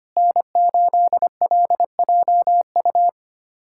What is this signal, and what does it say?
Morse code, heard as a keyed tone, about 25 wpm, 700 Hz. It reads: N8LJU
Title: USA Callsign DE Callsign 25wpm